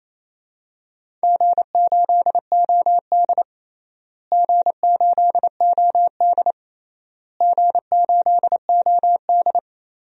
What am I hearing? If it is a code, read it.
Morse code, heard as a keyed tone, about 28 wpm, 700 Hz. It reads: G8OD G8OD G8OD
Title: DXCC-100 Call Signs - 4 Characters - Repeated 3x - 28wpm